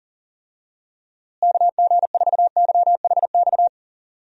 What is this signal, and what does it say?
Morse code, heard as a keyed tone, about 40 wpm, 700 Hz. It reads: KG4YHX
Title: USA Call Signs - 6 Characters - Repeated 3x - 40wpm